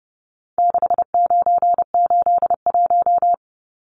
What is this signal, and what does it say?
Morse code, heard as a keyed tone, about 28 wpm, 700 Hz. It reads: 6981
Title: Numbers - 4-Digit 30wpm